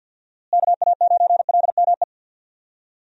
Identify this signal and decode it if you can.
Morse code, heard as a keyed tone, about 52 wpm, 700 Hz. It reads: KA9LME
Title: USA Call Signs - 6 Characters - Repeated 3x - 50wpm